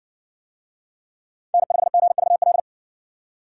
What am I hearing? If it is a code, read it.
Morse code, heard as a keyed tone, about 60 wpm, 700 Hz. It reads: N5GVL